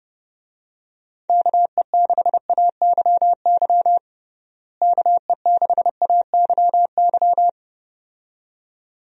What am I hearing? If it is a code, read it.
Morse code, heard as a keyed tone, about 30 wpm, 700 Hz. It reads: KE6AYY KE6AYY
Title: USA Call Signs - 6 Characters - Repeated 2x - 30wpm